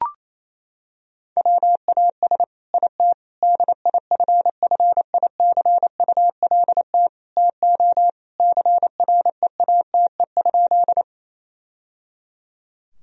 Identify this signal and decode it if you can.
Morse code, heard as a keyed tone, about 28 wpm, 700 Hz. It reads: WAS IT DIFFICULT TO CREATE?
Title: Sentences from Top 1000 Words - Review Part 3 - 28wpm